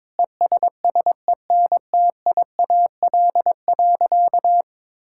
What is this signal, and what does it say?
Morse code, heard as a keyed tone, about 22 wpm, 700 Hz. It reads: ESSENTIAL.
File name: Sentences with 7-Word Phrases - 22wpm